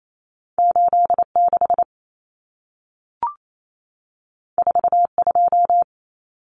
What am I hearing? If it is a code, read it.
Morse code, heard as a keyed tone, about 28 wpm, 700 Hz. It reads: 86 42
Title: Common Cut Numbers - 2-Digit 28wpm